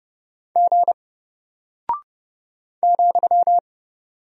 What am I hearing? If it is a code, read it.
Morse code, heard as a keyed tone, about 30 wpm, 700 Hz. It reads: G ,
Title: Single Character (Letter:Number:Punctuation:Prosign) Alt 30wpm